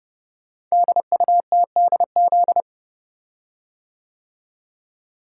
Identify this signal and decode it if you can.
Morse code, heard as a keyed tone, about 30 wpm, 700 Hz. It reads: DUTDZ
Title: Code Groups - 5 Letters - 30wpm